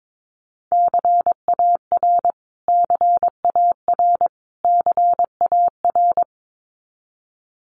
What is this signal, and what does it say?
Morse code, heard as a keyed tone, about 22 wpm, 700 Hz. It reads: CAR CAR CAR
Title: Top 500 Words Repeated 22wpm